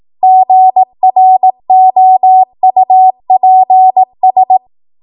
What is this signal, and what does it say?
Morse code, heard as a keyed tone, about 18 wpm, 750 Hz. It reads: GROUPS
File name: Morse / 18WPM